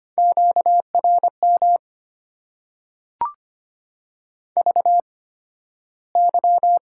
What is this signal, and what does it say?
Morse code, heard as a keyed tone, about 25 wpm, spaced as for 13 wpm, 690 Hz. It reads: QRM VY
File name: Top 90 QSO Elements - 8x Character Spacing - 25wpm